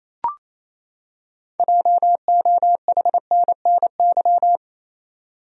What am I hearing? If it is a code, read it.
Morse code, heard as a keyed tone, about 28 wpm, 700 Hz. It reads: JOHNNY
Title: USA Common Names - 28wpm